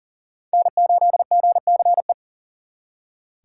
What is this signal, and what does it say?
Morse code, heard as a keyed tone, about 52 wpm, 700 Hz. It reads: N8GCE